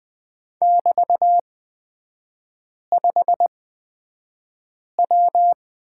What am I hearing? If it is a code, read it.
Morse code, heard as a keyed tone, about 20 wpm, spaced as for 5 wpm, 700 Hz. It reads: <BT>5W
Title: 224 Lesson 36e - BT - Three Characters 20wpm